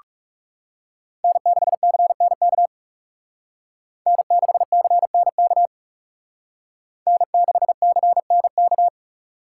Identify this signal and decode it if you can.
Morse code, heard as a keyed tone, about 36 wpm, 700 Hz. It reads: N6CNK N6CNK N6CNK
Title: Speed Racing - USA 1x3 Call Signs 30wpm